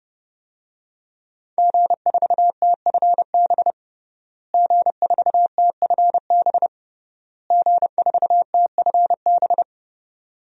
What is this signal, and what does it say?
Morse code, heard as a keyed tone, about 32 wpm, 700 Hz. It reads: G4TFB G4TFB G4TFB
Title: DXCC-100 Call Signs - 5 Characters - Repeated 3x - 30wpm